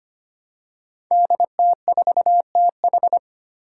Morse code, heard as a keyed tone, about 25 wpm, 700 Hz. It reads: DT4TH
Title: Five Challenging Letter-Number Combinations 25wpm